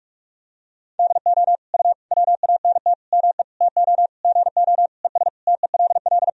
Morse code, heard as a keyed tone, about 42 wpm, 700 Hz. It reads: DO U WANT ME TO GO ES TELL
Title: Sentences from Top 200 Words and Abbreviations 45wpm